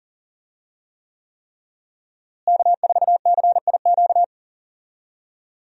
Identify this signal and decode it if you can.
Morse code, heard as a keyed tone, about 40 wpm, 700 Hz. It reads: K4CIQ
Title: USA Call Signs - 5 Characters - Repeated 3x - 40wpm